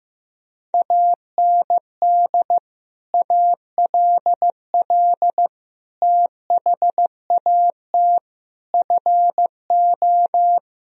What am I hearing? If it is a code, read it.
Morse code, heard as a keyed tone, about 15 wpm, 700 Hz. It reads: AND ALL THAT FO
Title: Sentences from Top 1000 Words - Review Part 4 - 15wpm